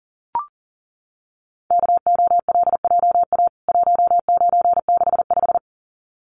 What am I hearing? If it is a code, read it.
Morse code, heard as a keyed tone, about 40 wpm, 700 Hz. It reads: KOLJA 1965
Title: CWT Exchange 40wpm